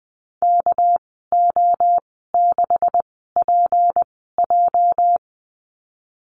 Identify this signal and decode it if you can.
Morse code, heard as a keyed tone, about 20 wpm, spaced as for 15 wpm, 700 Hz. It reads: KO6PJ
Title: 213 Lesson 34g - Number 8 - Call Signs 20wpm